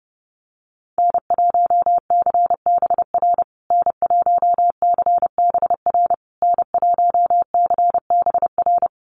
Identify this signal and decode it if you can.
Morse code, heard as a keyed tone, about 30 wpm, 700 Hz. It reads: N1CBR N1CBR N1CBR
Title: USA Call Signs Repeated 3x 30wpm